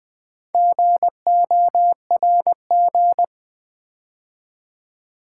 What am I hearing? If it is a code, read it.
Morse code, heard as a keyed tone, about 20 wpm, 700 Hz. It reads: GORG
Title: Build A Word - 2 Letters At A Time - 8-Letter Words - 20wpm